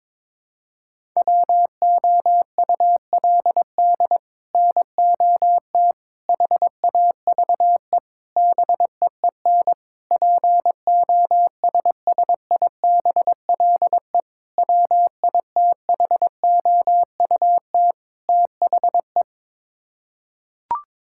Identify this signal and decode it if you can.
Morse code, heard as a keyed tone, about 22 wpm, 700 Hz. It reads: WOULD NOT HAVE BEEN POSSIBLE WITHOUT THE
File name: practice / 22wpm / Sets of 7 Words - 22wpm